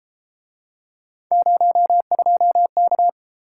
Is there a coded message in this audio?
Yes: Morse code from 1.3 to 3.1 s, heard as a keyed tone, about 31 wpm, 700 Hz. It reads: 02K